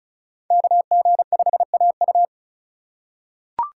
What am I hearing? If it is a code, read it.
Morse code, heard as a keyed tone, about 36 wpm, 700 Hz. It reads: KG5AU
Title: USA Call Signs - 5 Characters - 2x Character Spacing - 35wpm